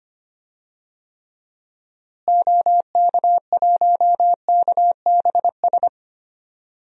Morse code, heard as a keyed tone, about 25 wpm, 700 Hz. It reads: OK1KBS
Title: DXCC-100 Call Signs - 6 Characters - Repeated 2x - 25wpm